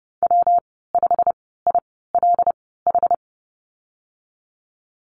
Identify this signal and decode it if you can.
Morse code, heard as a keyed tone, about 30 wpm, spaced as for 17 wpm, 700 Hz. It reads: W5ILH
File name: Lesson 19g - Letter w - Call Signs 30wpm